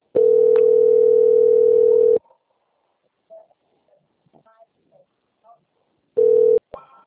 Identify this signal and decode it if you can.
Ringback tone (North America), 440 + 480 Hz, with speech.
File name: out- - - - - .7272